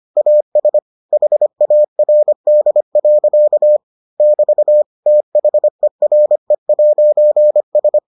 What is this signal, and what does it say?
Morse code, heard as a keyed tone, about 25 wpm, 600 Hz. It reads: AS HARD. <BT> THERE'S